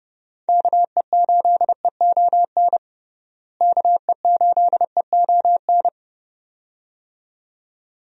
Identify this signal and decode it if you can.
Morse code, heard as a keyed tone, about 32 wpm, 700 Hz. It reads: KE8EON KE8EON
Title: USA Call Signs - 6 Characters - Repeated 2x - 30wpm